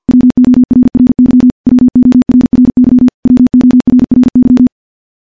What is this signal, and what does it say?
Morse code, heard as a keyed tone, about 46 wpm, 250 Hz. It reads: WORRY WORRY WORRY
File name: morse / morse-training-50WPM